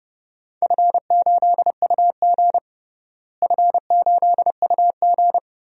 Morse code, heard as a keyed tone, about 32 wpm, 700 Hz. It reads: F8UG F8UG
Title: DXCC-100 Call Signs - 4 Characters - Repeated 2x - 30wpm